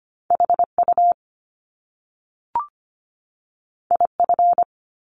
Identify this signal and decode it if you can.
Morse code, heard as a keyed tone, about 25 wpm, 700 Hz. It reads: HU IF
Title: Top Two Letter Combinations in Words 25wpm